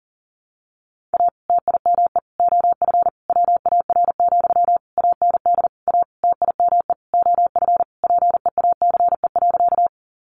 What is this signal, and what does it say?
Morse code, heard as a keyed tone, about 40 wpm, 700 Hz. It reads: A TIME OF WAR, AND A TIME OF PEACE.
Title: Sentences from Top 1500 Words 40wpm